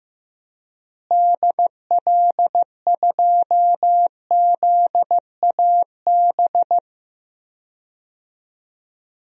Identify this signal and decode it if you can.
Morse code, heard as a keyed tone, about 15 wpm, 700 Hz. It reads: DL2ZAB
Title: DXCC-100 Call Signs - 6 Characters - Repeated 3x - 15wpm